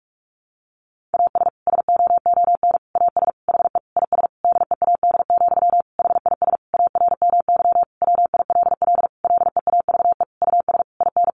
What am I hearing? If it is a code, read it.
Morse code, heard as a keyed tone, about 44 wpm, 700 Hz. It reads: AS SOON AS HE IS DEAD, HIS ARMY WILL LEAVE US IN
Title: Sentences from Top 1500 Words 45wpm